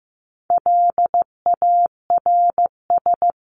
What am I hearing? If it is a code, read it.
Morse code, heard as a keyed tone, about 14 wpm, 700 Hz. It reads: LARS
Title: CWT Exchange 15wpm